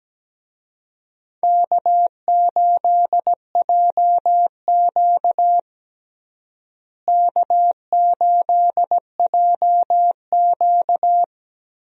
Morse code, heard as a keyed tone, about 17 wpm, 700 Hz. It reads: K8JQ K8JQ
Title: USA Call Signs - 4 Characters - Repeated 2x - 17wpm